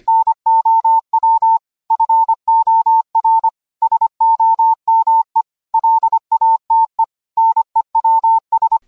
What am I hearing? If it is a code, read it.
Morse code, heard as a keyed tone, about 25 wpm, 900 Hz. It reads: NOW FOR SOME LATE NEWS